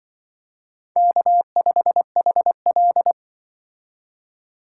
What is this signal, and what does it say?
Morse code, heard as a keyed tone, about 25 wpm, 700 Hz. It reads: K5HL
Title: Speed Racing - USA 1x2 Call Signs 20wpm